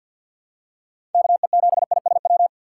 Morse code, heard as a keyed tone, about 49 wpm, 700 Hz. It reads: KE7ISW
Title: USA Call Signs - 6 Characters - 4x Character Spacing - 50wpm